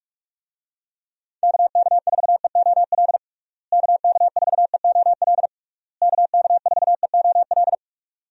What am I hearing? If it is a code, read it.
Morse code, heard as a keyed tone, about 46 wpm, 700 Hz. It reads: KK4EOL KK4EOL KK4EOL
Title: USA Call Signs - 6 Characters - Repeated 3x - 45wpm